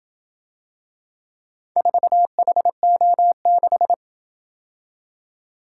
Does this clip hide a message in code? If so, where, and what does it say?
Yes: Morse code from 1.8 to 3.9 s, heard as a keyed tone, about 27 wpm, 700 Hz. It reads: 4HO6